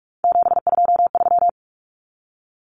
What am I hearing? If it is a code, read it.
Morse code, heard as a keyed tone, about 44 wpm, 700 Hz. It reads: 723
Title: Common Cut Numbers - 3-Digit 45wpm